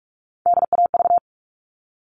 Morse code, heard as a keyed tone, about 44 wpm, 700 Hz. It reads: DAV
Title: All Cut Numbers - 3-Digit 45wpm